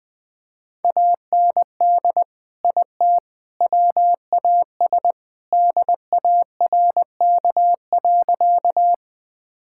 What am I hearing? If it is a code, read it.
Morse code, heard as a keyed tone, about 20 wpm, 700 Hz. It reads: AND IT WAS DARK.
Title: Sentences with 3-Word Phrases - 20wpm